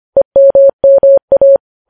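Morse code, heard as a keyed tone, about 25 wpm, 550 Hz. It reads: EMMA
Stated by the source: Morse code news headlines sent at 25 wpm.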